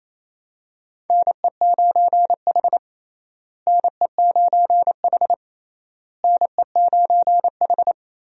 Morse code, heard as a keyed tone, about 28 wpm, 700 Hz. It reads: NE9H NE9H NE9H
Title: USA Call Signs - 4 Characters - Repeated 3x - 28wpm